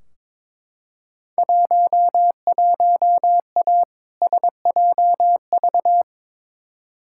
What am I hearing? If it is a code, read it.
Morse code, heard as a keyed tone, about 22 wpm, 700 Hz. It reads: 11A SJV
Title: Field Day Exchanges Encore - 22wpm